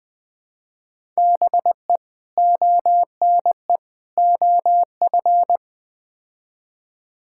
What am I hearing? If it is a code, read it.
Morse code, heard as a keyed tone, about 20 wpm, 700 Hz. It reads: BE ONE OF
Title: Build A Phrase - 4-Word Phrases - 20wpm